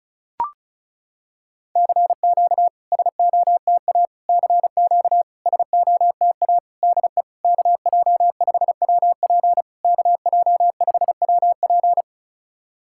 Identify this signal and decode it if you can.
Morse code, heard as a keyed tone, about 36 wpm, 690 Hz. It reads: CQ SOTA CQ SOTA DE KJ5WP KJ5WP
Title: SOTA QSO Practice 35wpm